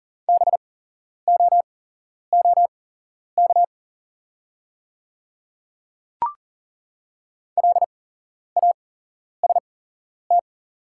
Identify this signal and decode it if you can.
Morse code, heard as a keyed tone, about 40 wpm, spaced as for 11 wpm, 690 Hz. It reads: BOOK LAST
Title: 8x Character Spacing - Top 500 Words - 40wpm